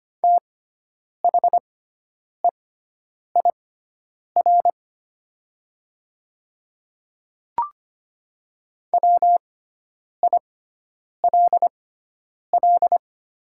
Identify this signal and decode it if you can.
Morse code, heard as a keyed tone, about 25 wpm, spaced as for 9 wpm, 700 Hz. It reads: THEIR WILL
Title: 6x Character Spacing - Top 100 Words - 25wpm